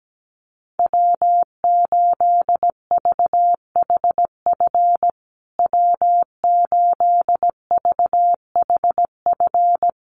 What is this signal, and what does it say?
Morse code, heard as a keyed tone, about 17 wpm, 700 Hz. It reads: W8VHF W8VHF
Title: USA Call Signs Repeated 2x 17wpm